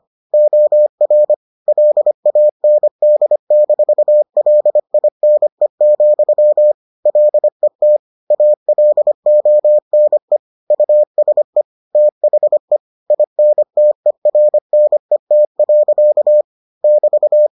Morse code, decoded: OR LAND-LINE, LET ALONE USE THE INTERNET. <BT>

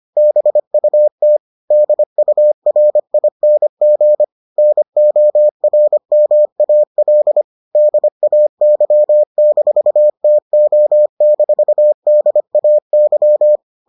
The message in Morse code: BUT DURING NORMAL DAY-TO-DAY

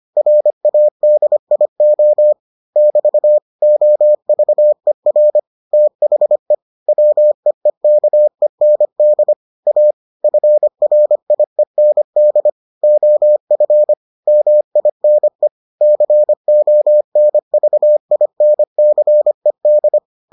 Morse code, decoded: RADIO <BT> OVER THE WEEKEND A FRIEND OF MINE CONVINCED

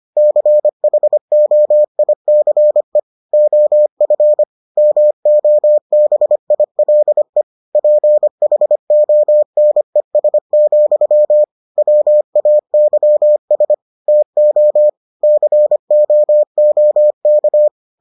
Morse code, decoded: CHOICE OF MOBILE PHONES, WAYS TO COOK

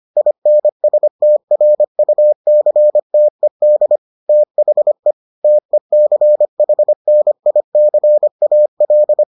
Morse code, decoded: INSTRUCTED THE TECHNICAL

